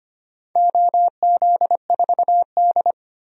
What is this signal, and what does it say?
Morse code, heard as a keyed tone, about 25 wpm, 700 Hz. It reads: OZ4D
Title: DXCC-100 Call Signs - 4 Characters - 25wpm